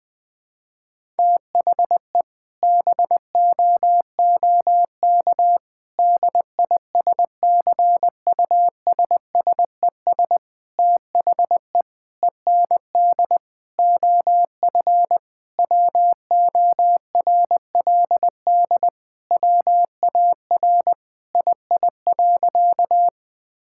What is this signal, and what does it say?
Morse code, heard as a keyed tone, about 19 wpm, 700 Hz. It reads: THE BOOK DISCUSSES THE END OF WORLD WAR II.
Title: Sentences with 6-Word Phrases - Spoken Beginning - 20wpm